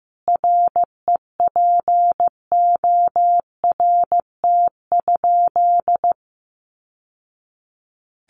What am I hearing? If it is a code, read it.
Morse code, heard as a keyed tone, about 15 wpm, 700 Hz. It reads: REPORT?
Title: Sentences from Top 1200 Words 15wpm